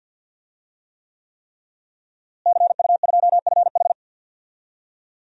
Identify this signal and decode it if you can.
Morse code, heard as a keyed tone, about 51 wpm, 700 Hz. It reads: CU2FH
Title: DXCC-100 Call Signs - 5 Characters - Repeated 3x - 50wpm